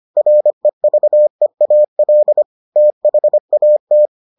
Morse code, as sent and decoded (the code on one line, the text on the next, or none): .-. ...- . .- .-.. / - .... .- -
REVEAL THAT